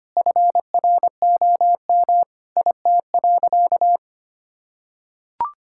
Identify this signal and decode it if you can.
Morse code, heard as a keyed tone, about 25 wpm, 700 Hz. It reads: FROM IT.
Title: Sentences from Top 1000 Words - Review Part 5 - 25wpm